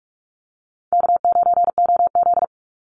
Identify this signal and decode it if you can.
Morse code, heard as a keyed tone, about 44 wpm, 700 Hz. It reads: K9OZ
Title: Easy Call Signs - 45wpm